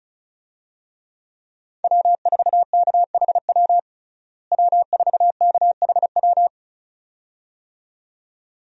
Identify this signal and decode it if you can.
Morse code, heard as a keyed tone, about 36 wpm, 700 Hz. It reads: W4KHW W4KHW
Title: USA Call Signs - 5 Characters - Repeated 2x - 35wpm